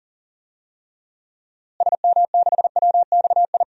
Morse code, heard as a keyed tone, about 37 wpm, 700 Hz. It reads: SM6WXI
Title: DXCC-100 Call Signs - 6 Characters - Repeated 2x - 40wpm